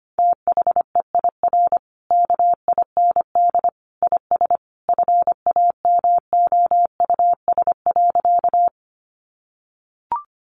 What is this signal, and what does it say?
Morse code, heard as a keyed tone, about 25 wpm, 700 Hz. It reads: THEIR KIND IS FAMOUS.
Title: Sentences from Top 1500 Words 25wpm